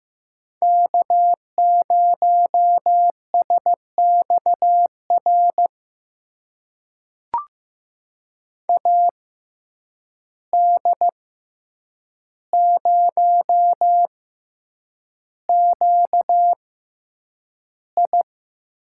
Morse code, heard as a keyed tone, about 15 wpm, spaced as for 8 wpm, 700 Hz. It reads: K0SXR AD0QI
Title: USA Call Signs - 5 Characters - 6x Character Spacing - 15wpm